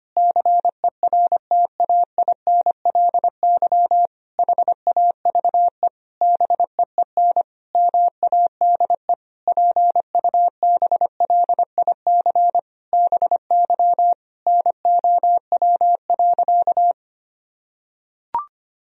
Morse code, heard as a keyed tone, about 25 wpm, 700 Hz. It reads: CERTAINLY HAVE BEEN MADE PUBLIC BY NOW.
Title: Sentences from Top 1000 Words - Review Part 3 - 25wpm